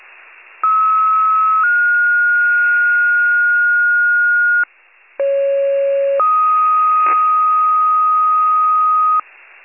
Two-tone page, heard as a one-second tone, then a three-second tone, about 1290 then 1430 Hz, about 550 then 1250 Hz. It is heard from 0.6 to 9.2 s.